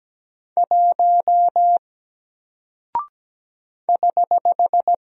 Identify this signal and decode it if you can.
Morse code, heard as a keyed tone, about 17 wpm, 700 Hz. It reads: 1 <HH>
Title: Single Character (Letter:Number:Punctuation:Prosign) Alt 17wpm